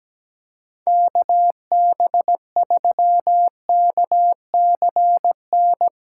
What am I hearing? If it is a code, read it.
Morse code, heard as a keyed tone, about 17 wpm, 700 Hz. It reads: KB3KCN
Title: USA Callsign DE Callsign - 6 Characters - 17wpm